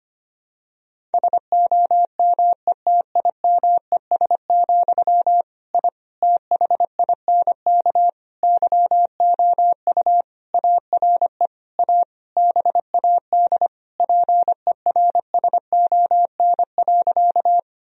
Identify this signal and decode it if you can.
Morse code, heard as a keyed tone, about 25 wpm, 700 Hz. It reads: SOMETIMES, I THINK YOU ARE A BAD PERSON.
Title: Sentences from Top 1000 Words - Review Part 3 - 25wpm